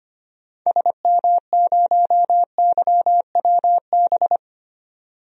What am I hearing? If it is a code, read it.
Morse code, heard as a keyed tone, about 25 wpm, 700 Hz. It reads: SM0YWB
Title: DXCC-100 Call Signs - 6 Characters - 25wpm